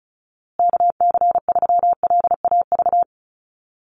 Morse code, heard as a keyed tone, about 34 wpm, 700 Hz. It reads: KC3LAV
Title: USA Call Signs Repeated 3x 35wpm